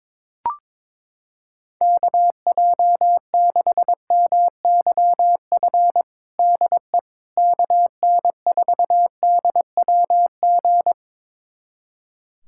Morse code, heard as a keyed tone, about 22 wpm, 700 Hz. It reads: KJ6MYF DE KN4DWG
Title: USA Callsign DE Callsign 22wpm